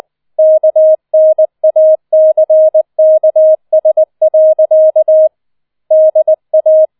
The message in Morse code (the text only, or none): KNACKS. DA